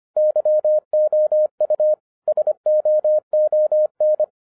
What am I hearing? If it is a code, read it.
Morse code, heard as a keyed tone, about 25 wpm, 610 Hz. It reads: YOU SOON